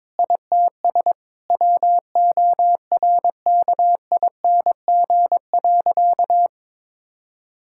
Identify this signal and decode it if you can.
Morse code, heard as a keyed tone, about 21 wpm, 700 Hz. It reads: ITS WORKING.
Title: Sentences from Top 1000 Words - Review Part 6 - 22wpm